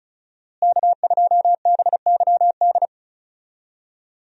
Morse code, heard as a keyed tone, about 36 wpm, 700 Hz. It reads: K2BYD
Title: USA Call Signs - 5 Characters - Repeated 2x - 35wpm